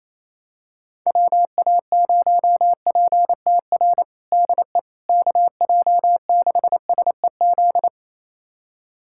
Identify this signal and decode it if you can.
Morse code, heard as a keyed tone, about 28 wpm, 700 Hz. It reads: WA0PTR DE KJ6SEZ
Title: USA Callsign DE Callsign - 6 Characters - 28wpm